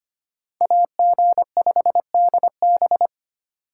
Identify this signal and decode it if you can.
Morse code, heard as a keyed tone, about 33 wpm, 700 Hz. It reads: AG5DB